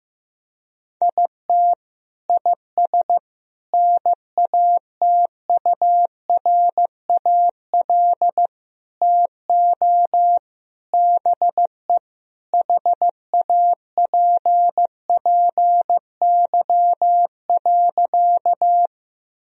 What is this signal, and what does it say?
Morse code, heard as a keyed tone, about 15 wpm, 700 Hz. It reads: IT IS NATURAL TO BE HAPPY.
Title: Sentences from Top 1000 Words - Review Part 4 - 15wpm